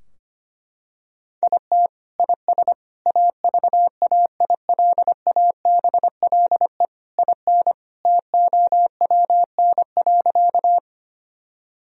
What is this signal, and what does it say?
Morse code, heard as a keyed tone, about 25 wpm, 700 Hz. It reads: IT IS AVAILABLE IN TOWN.
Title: Sentences from Top 1000 Words - Review Part 3 - 25wpm